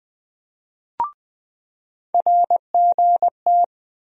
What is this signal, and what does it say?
Morse code, heard as a keyed tone, about 20 wpm, 700 Hz. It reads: RGT
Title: USA Call Sign Postfixes - 3 Letters 20wpm